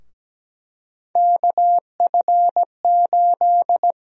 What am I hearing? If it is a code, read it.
Morse code, heard as a keyed tone, about 17 wpm, 700 Hz. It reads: KF8
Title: USA Call Sign Prefixes - 17wpm